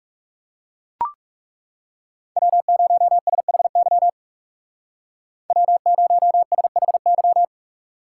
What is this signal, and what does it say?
Morse code, heard as a keyed tone, about 39 wpm, 700 Hz. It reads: W0SHY W0SHY